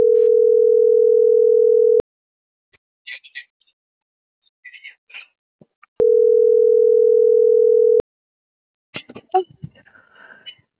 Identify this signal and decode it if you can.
Ringback tone (North America), 440 + 480 Hz.